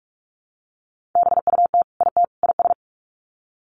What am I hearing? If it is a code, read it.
Morse code, heard as a keyed tone, about 44 wpm, 700 Hz. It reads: BUT IT IS
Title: Sets of 3 Words 45wpm